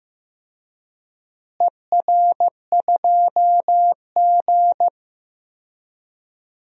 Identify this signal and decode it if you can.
Morse code, heard as a keyed tone, about 15 wpm, 700 Hz. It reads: ER2G
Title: DXCC-100 Call Signs - 4 Characters - Repeated 2x - 15wpm